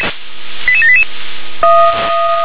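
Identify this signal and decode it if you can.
Five-tone selective call (ZVEI), heard as quick runs of five tones, tones of about 70 ms. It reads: 88788
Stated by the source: ZVEI five-tone call received by SDR.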